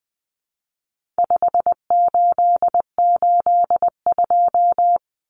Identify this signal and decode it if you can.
Morse code, heard as a keyed tone, about 20 wpm, 700 Hz. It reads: 5882
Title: Numbers - 4-Digit 20wpm